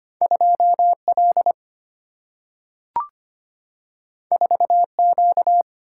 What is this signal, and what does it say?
Morse code, heard as a keyed tone, about 25 wpm, 700 Hz. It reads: 2 L 4 Q
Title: Single-Letter Single-Number 25wpm